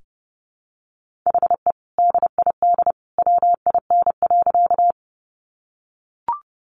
Morse code, heard as a keyed tone, about 30 wpm, 700 Hz. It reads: HE DID WIN.
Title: Sentences from Top 1500 Words 30wpm